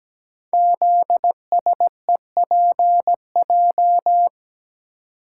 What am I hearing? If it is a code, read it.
Morse code, heard as a keyed tone, about 17 wpm, 700 Hz. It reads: ZSEPJ
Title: Code Groups - 5 Letters - 17wpm